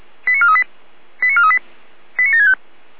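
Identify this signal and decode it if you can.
Five-tone selective call (ZVEI), heard as quick runs of five tones, tones of about 80 ms. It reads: 89438 89438 89765
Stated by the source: Received by SDR, ZVEI five-tone call.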